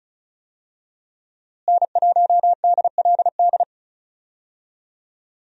Morse code, heard as a keyed tone, about 34 wpm, 700 Hz. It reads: N1DLD
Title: USA Call Signs - 5 Characters - Repeated 2x - 35wpm